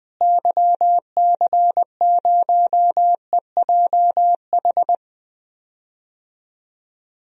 Morse code, heard as a keyed tone, about 20 wpm, 700 Hz. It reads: YC0EJH
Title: DXCC-100 Call Signs - 6 Characters - Repeated 3x - 20wpm